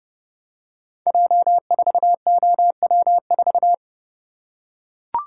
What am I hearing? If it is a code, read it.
Morse code, heard as a keyed tone, about 30 wpm, 700 Hz. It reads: J4OW4
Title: Code Groups - 5 Letter:Numbers - 30wpm